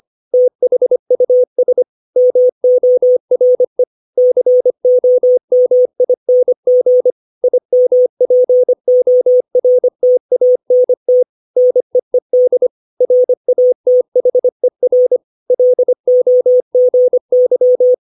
Morse code, heard as a keyed tone, about 25 wpm, 500 Hz. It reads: THUS MORE COMING IMPORTANT NEED RATHER LOGY
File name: 632 words / 632 Words 25WPM